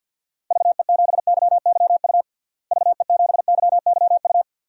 Morse code, heard as a keyed tone, about 49 wpm, 700 Hz. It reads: VE7YYU VE7YYU
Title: DXCC-100 Call Signs - 6 Characters - Repeated 2x - 50wpm